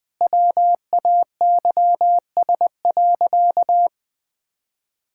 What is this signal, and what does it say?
Morse code, heard as a keyed tone, about 20 wpm, 700 Hz. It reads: WAYS.
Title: Sentences with 2-Word Phrases - 20wpm